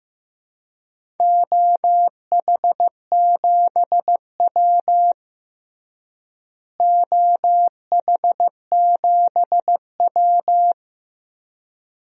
Morse code, heard as a keyed tone, about 16 wpm, 700 Hz. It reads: OH7W OH7W
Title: DXCC-100 Call Signs - 4 Characters - Repeated 2x - 15wpm